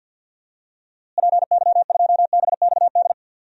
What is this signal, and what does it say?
Morse code, heard as a keyed tone, about 53 wpm, 700 Hz. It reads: PY2BXD
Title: DXCC-100 Call Signs - 6 Characters - Repeated 2x - 50wpm